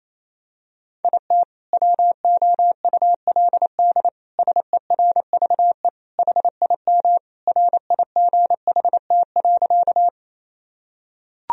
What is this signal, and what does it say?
Morse code, heard as a keyed tone, about 28 wpm, 700 Hz. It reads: IT WOULD SERVE HIM RIGHT.
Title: Sentences from Top 1000 Words - Review Part 2 - 28wpm